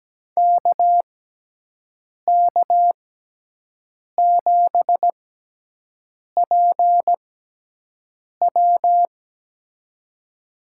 Morse code, heard as a keyed tone, about 17 wpm, spaced as for 6 wpm, 700 Hz. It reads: KK7PW
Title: USA Call Signs - 5 Characters - 6x Character Spacing - 17wpm